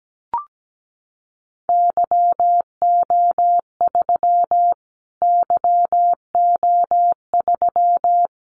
Morse code, heard as a keyed tone, about 17 wpm, 700 Hz. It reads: YO3 YO3
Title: Callsign Prefixes for Top 25 DXCC - Repeated 3x - 17wpm